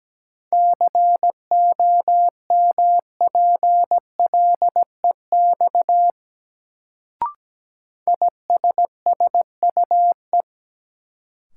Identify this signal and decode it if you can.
Morse code, heard as a keyed tone, about 17 wpm, 700 Hz. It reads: COMPLEX ISSUE
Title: Sentences from Top 900 Words - New Word Drill - 17wpm